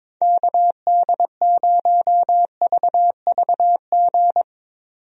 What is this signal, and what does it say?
Morse code, heard as a keyed tone, about 22 wpm, 700 Hz. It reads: KD0VVG
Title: USA Call Signs - 6 Characters - 22wpm